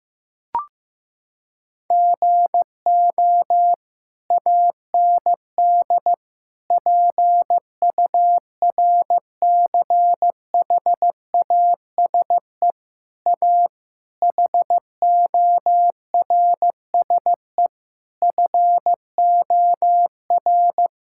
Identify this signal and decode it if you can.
Morse code, heard as a keyed tone, about 16 wpm, 700 Hz. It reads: GO AND PURCHASE A HORSE FOR
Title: Sentences from Top 1000 Words - Review Part 5 - 15wpm